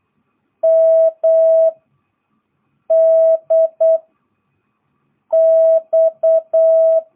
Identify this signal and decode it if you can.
Morse code, heard as a keyed tone, about 8 wpm, 660 Hz. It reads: M D X
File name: jh8cbh / 2022cw